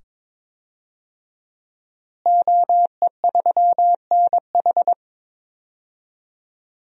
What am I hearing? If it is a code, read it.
Morse code, heard as a keyed tone, about 22 wpm, 700 Hz. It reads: OE3NH
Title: DXCC-100 Call Signs - 5 Characters - Repeated 2x - 22wpm